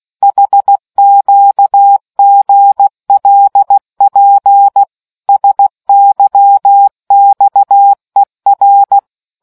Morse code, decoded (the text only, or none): HQGLP SYXER